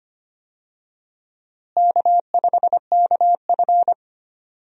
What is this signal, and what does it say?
Morse code, heard as a keyed tone, about 25 wpm, 700 Hz. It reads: K5KF
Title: USA Call Signs - 4 Characters - Repeated 3x - 25wpm